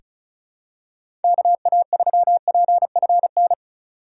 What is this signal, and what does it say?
Morse code, heard as a keyed tone, about 34 wpm, 700 Hz. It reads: KA3PFN
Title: USA Call Signs - 6 Characters - 4x Character Spacing - 35wpm